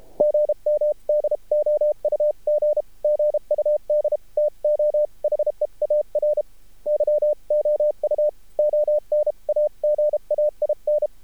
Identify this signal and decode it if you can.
Morse code, heard as a keyed tone, about 33 wpm, 600 Hz. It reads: GM DOUG GUD TO HEAR YOU ON AGAIN
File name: archived / CRSnet